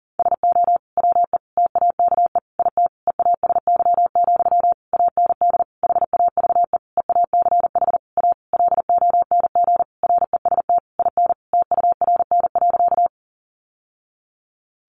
Morse code, heard as a keyed tone, about 40 wpm, 700 Hz. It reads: SO WE TAKE IT EASY, AND HAVE EACH A LONG REST IN TURN.
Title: Sentences from Top 700 Words 40wpm